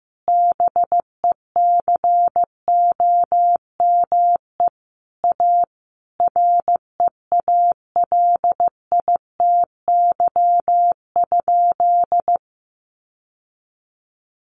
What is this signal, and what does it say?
Morse code, heard as a keyed tone, about 14 wpm, 700 Hz. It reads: BECOME A REALITY?
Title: Sentences from Top 1200 Words 15wpm